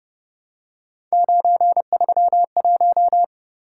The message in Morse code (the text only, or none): 931